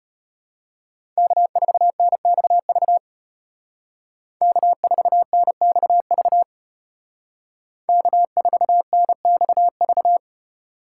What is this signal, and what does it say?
Morse code, heard as a keyed tone, about 30 wpm, 700 Hz. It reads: K4NXV K4NXV K4NXV
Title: Speed Racing - USA 1x3 Call Signs 25wpm